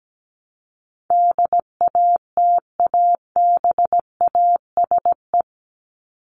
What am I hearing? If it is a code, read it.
Morse code, heard as a keyed tone, about 17 wpm, 700 Hz. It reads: DATABASE
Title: Top 1751-2000 Words 17wpm